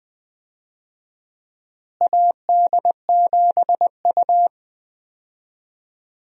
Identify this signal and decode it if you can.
Morse code, heard as a keyed tone, about 20 wpm, 700 Hz. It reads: AD7U
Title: USA Call Signs - 4 Characters - Repeated 3x - 20wpm